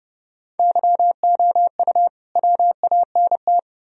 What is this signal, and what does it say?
Morse code, heard as a keyed tone, about 30 wpm, 700 Hz. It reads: YOU WANT